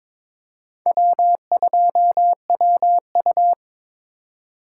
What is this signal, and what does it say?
Morse code, heard as a keyed tone, about 22 wpm, 700 Hz. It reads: W2WU
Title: USA Call Signs - 3x Character Spacing - 22wpm